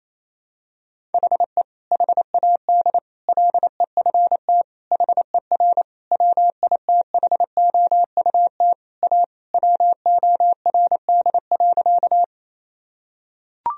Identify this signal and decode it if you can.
Morse code, heard as a keyed tone, about 28 wpm, 700 Hz. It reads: HE HAD LEFT HER WITHOUT A WORD.
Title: Sentences from Top 1000 Words - Review Part 3 - 28wpm